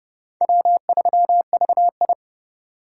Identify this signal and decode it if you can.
Morse code, heard as a keyed tone, about 29 wpm, 700 Hz. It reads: W3VI